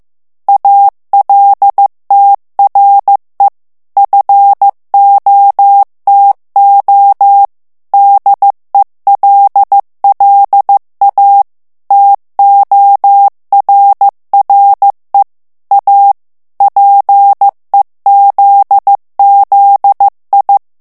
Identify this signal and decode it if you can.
Morse code, heard as a keyed tone, about 15 wpm, 800 Hz. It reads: ALTRE FOTO DELLA TORRE A PEZZI
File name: ITALIEN texte clair 1 vitesse 14 mots minute